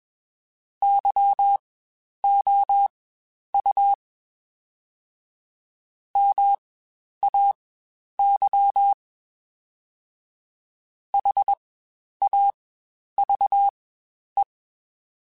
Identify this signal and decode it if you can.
Morse code, heard as a keyed tone, about 21 wpm, spaced as for 10 wpm, 800 Hz. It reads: YOU MAY HAVE
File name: Spaced / Chuckles Spaced-mp3s / SoftWords sp @20